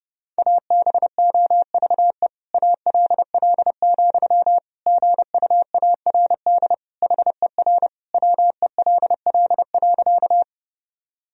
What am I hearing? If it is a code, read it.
Morse code, heard as a keyed tone, about 28 wpm, 700 Hz. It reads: ABOVE ALL, GUARD HER WELL.
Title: Sentences from Top 1000 Words - Review Part 4 - 30wpm